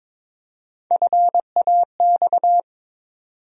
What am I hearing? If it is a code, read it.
Morse code, heard as a keyed tone, about 22 wpm, 700 Hz. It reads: FAX
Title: USA Call Sign Postfixes - 3 Letters 22wpm